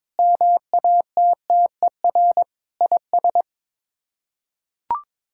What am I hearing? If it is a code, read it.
Morse code, heard as a keyed tone, about 22 wpm, 700 Hz. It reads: MATTER IS
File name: practice / 22wpm / Build A Phrase - 5-Word Phrases - 22wpm